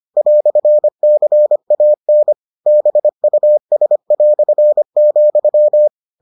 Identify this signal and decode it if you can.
Morse code, heard as a keyed tone, about 25 wpm, 600 Hz. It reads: "CAN BUS",